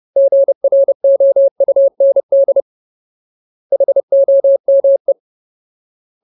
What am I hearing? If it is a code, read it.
Morse code, heard as a keyed tone, about 30 wpm, 550 Hz. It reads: GROUND HOME
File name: Ultimate Word List / 30WPM Ultimate